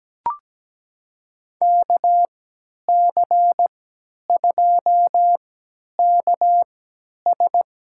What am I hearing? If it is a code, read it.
Morse code, heard as a keyed tone, about 17 wpm, spaced as for 10 wpm, 700 Hz. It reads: KC2KS
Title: USA Call Signs - 3x Character Spacing - 17wpm